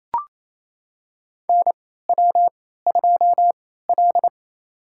Morse code, heard as a keyed tone, about 28 wpm, spaced as for 16 wpm, 700 Hz. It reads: NW2L
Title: USA Call Signs - 3x Character Spacing - 28wpm